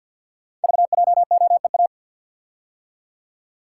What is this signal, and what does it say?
Morse code, heard as a keyed tone, about 52 wpm, 700 Hz. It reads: VJOEA
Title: Code Groups - 5 Letters - 50wpm